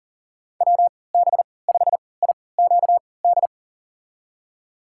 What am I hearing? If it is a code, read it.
Morse code, heard as a keyed tone, about 40 wpm, spaced as for 23 wpm, 700 Hz. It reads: WB5IQD
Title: USA Call Signs - 6 Characters - 3x Character Spacing - 40wpm